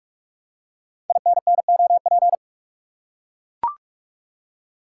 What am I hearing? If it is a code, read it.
Morse code, heard as a keyed tone, about 44 wpm, 700 Hz. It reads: INNOP
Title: Code Groups - 5 Letters - 45wpm